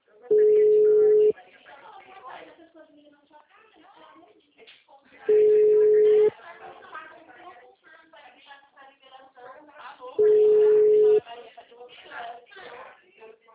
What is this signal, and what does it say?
Ringback tone (Europe, 425 Hz), with speech.